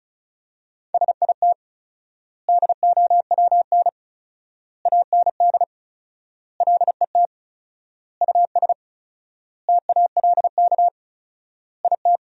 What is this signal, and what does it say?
Morse code, heard as a keyed tone, about 36 wpm, 700 Hz. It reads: SIT DOWN AND LET US TALK IT
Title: 4x Word-Spacing - Sentences from Top 500 Words - 35wpm